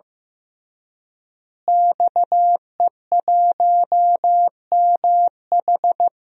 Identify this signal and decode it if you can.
Morse code, heard as a keyed tone, about 15 wpm, 700 Hz. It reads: XE1MH
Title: DXCC-100 Call Signs - 5 Characters - Repeated 3x - 15wpm